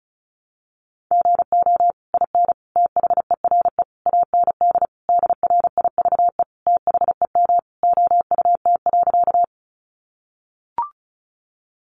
Morse code, heard as a keyed tone, about 34 wpm, 700 Hz. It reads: GO IN THERE AND DRIVE THEM OUT.
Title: Sentences from Top 500 Words 35wpm